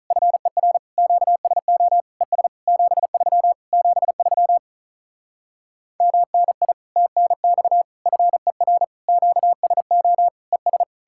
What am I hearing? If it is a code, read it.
Morse code, heard as a keyed tone, about 46 wpm, 700 Hz. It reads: FER QSO ES 73 73 MNI TNX FER QSO ES